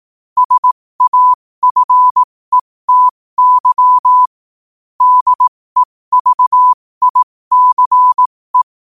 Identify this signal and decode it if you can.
Morse code, heard as a keyed tone, about 18 wpm, spaced as for 15 wpm, 1000 Hz. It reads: SAFETY DEVICE
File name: qotdfiles / 15wpm